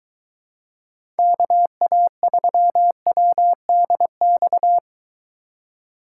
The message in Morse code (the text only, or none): KA3WDX